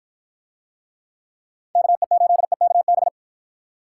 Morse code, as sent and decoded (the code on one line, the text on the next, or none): -.- . ---.. -.- -...
KE8EKB